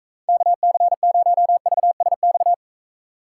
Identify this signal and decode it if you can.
Morse code, heard as a keyed tone, about 44 wpm, 700 Hz. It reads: KC0VSX